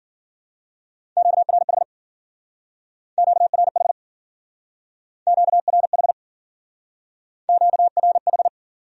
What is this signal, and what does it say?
Morse code, heard as a keyed tone, about 52 wpm, 700 Hz. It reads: QRH QRH QRH QRH